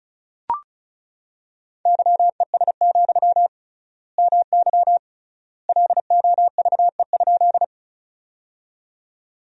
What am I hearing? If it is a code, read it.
Morse code, heard as a keyed tone, about 36 wpm, 700 Hz. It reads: YES, MY LOVE?
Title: 3x Word-Spacing - Sentences from Top 500 Words - 35wpm